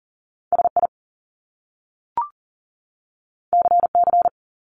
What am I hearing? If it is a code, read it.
Morse code, heard as a keyed tone, about 39 wpm, 700 Hz. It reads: SI CC
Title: Top Two Letter Combinations in Words 40wpm